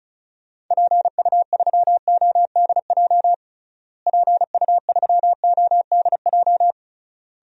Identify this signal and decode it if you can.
Morse code, heard as a keyed tone, about 36 wpm, 700 Hz. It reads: PU3ODJ PU3ODJ
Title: DXCC-100 Call Signs - 6 Characters - Repeated 3x - 35wpm